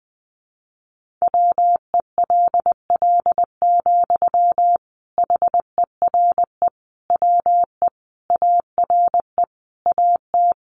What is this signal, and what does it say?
Morse code, heard as a keyed tone, about 20 wpm, 700 Hz. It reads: WELL, HERE WE ARE AT
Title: Sentences from Top 300 Words 20wpm